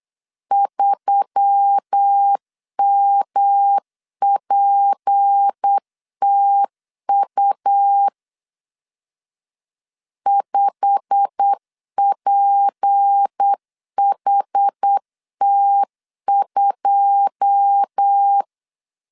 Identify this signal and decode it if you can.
Morse code, heard as a keyed tone, about 8 wpm, 800 Hz. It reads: 3MPTU 5PHT2
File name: LEZIONI / 9 2 8wpm 3437